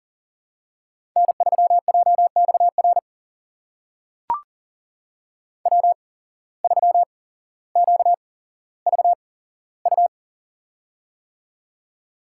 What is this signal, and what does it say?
Morse code, heard as a keyed tone, about 39 wpm, spaced as for 17 wpm, 700 Hz. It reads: N3JXR W3QVU
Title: USA Call Signs - 5 Characters - 8x Character Spacing - 40wpm